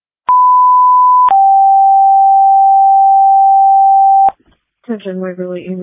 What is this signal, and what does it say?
Two-tone page, heard as a one-second tone, then a three-second tone, about 1000 then 780 Hz, with speech. It is heard from 0.3 to 4.3 s.